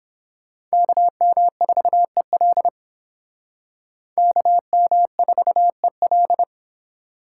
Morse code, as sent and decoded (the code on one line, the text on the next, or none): -.- -- ....- . .-.. / -.- -- ....- . .-..
KM4EL KM4EL